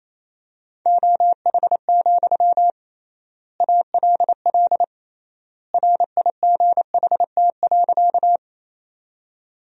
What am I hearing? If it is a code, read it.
Morse code, heard as a keyed tone, about 28 wpm, 700 Hz. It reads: OH, ALL RIGHT.
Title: 3x Word-Spacing - Sentences from Top 500 Words - 28wpm